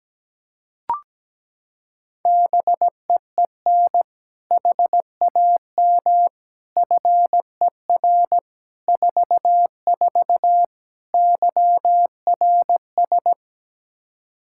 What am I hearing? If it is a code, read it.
Morse code, heard as a keyed tone, about 17 wpm, 700 Hz. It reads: BEEN HAM FER 44 YRS
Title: Practice QSO - Exchange 3 - More Info - 17wpm